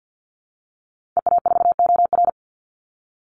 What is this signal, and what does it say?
Morse code, heard as a keyed tone, about 50 wpm, 700 Hz. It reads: EA4OR
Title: Call Signs - 50wpm